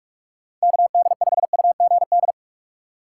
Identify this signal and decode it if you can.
Morse code, heard as a keyed tone, about 47 wpm, 700 Hz. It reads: KD5UGD
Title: USA Call Signs - 4x Character Spacing - 45wpm